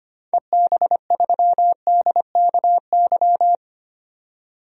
Morse code, heard as a keyed tone, about 25 wpm, 700 Hz. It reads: EB3DKY
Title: DXCC-100 Call Signs - 6 Characters - 25wpm